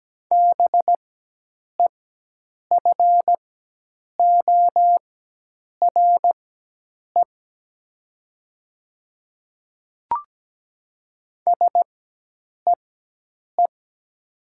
Morse code, heard as a keyed tone, about 17 wpm, spaced as for 8 wpm, 700 Hz. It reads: BEFORE SEE
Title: 4x Character Spacing - Top 200 Words - 17wpm